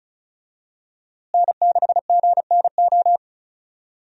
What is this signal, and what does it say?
Morse code, heard as a keyed tone, about 36 wpm, 700 Hz. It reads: N6GNO
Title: USA Call Signs - 5 Characters - Repeated 3x - 35wpm